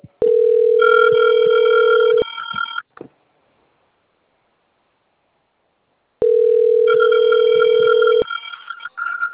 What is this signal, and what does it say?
Ringback tone (North America), 440 + 480 Hz.
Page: Sao Carlos